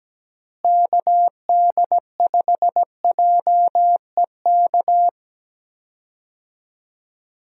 Morse code, heard as a keyed tone, about 17 wpm, 700 Hz. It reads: KD5JEK
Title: USA Call Signs - 6 Characters - Repeated 3x - 17wpm